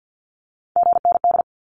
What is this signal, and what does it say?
Morse code, heard as a keyed tone, about 50 wpm, 700 Hz. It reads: GND
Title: Abbreviations 50wpm